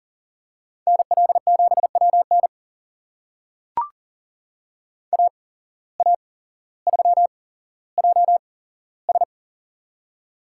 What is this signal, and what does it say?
Morse code, heard as a keyed tone, about 39 wpm, spaced as for 17 wpm, 700 Hz. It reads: NL7WN AA3JS
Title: USA Call Signs - 5 Characters - 8x Character Spacing - 40wpm